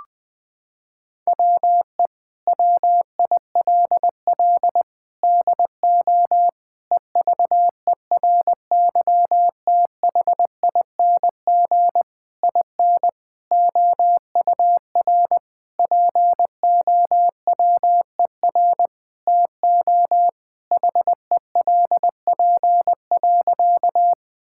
Morse code, heard as a keyed tone, about 20 wpm, 700 Hz. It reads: WE WILL DO EVERYTHING IN OUR POWER TO HELP.
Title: Sentences with 8-Word Phrases - 20wpm